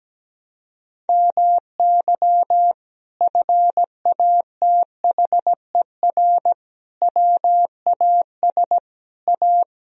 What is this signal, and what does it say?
Morse code, heard as a keyed tone, about 17 wpm, 700 Hz. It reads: MY FATHER WAS A
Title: Sentences from Top 1000 Words - Review Part 1 - 17wpm